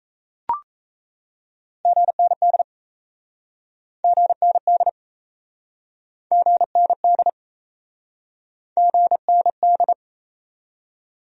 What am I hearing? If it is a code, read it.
Morse code, heard as a keyed tone, about 37 wpm, 700 Hz. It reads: GND GND GND GND